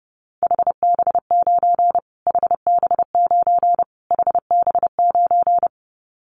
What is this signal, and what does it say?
Morse code, heard as a keyed tone, about 30 wpm, 700 Hz. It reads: HB9 HB9 HB9
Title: Callsign Prefixes for Top 25 DXCC - Repeated 3x - 30wpm